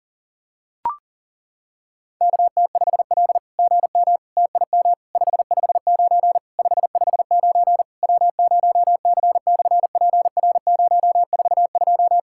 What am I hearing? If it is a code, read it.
Morse code, heard as a keyed tone, about 40 wpm, 690 Hz. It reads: KT5L GM TIM 559 559 W0C/PR042
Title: SOTA QSO Practice 40wpm